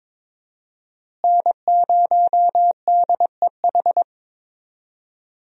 Morse code, heard as a keyed tone, about 22 wpm, 700 Hz. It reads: N0DEH
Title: USA Call Signs - 5 Characters - Repeated 3x - 22wpm